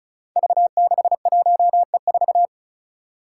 Morse code, heard as a keyed tone, about 34 wpm, 700 Hz. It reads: V61E4
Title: Five Challenging Letter-Number Combinations 35wpm